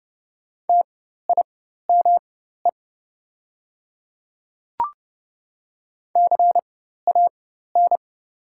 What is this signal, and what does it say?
Morse code, heard as a keyed tone, about 30 wpm, spaced as for 14 wpm, 700 Hz. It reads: TIME CAN
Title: 4x Character Spacing - Top 100 Words - 30wpm